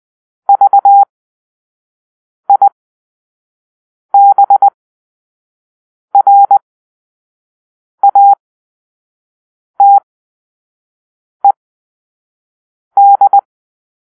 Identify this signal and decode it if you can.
Morse code, heard as a keyed tone, about 20 wpm, spaced as for 5 wpm, 800 Hz. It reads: VIBRATED
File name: Grp-5 Drill-2